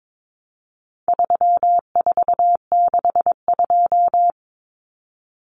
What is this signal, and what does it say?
Morse code, heard as a keyed tone, about 22 wpm, 700 Hz. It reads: 3462
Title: Common Cut Numbers - 4-Digit 22wpm